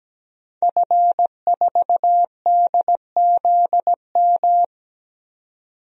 Morse code, heard as a keyed tone, about 17 wpm, 700 Hz. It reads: F4DZM
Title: DXCC-100 Call Signs - 5 Characters - 17wpm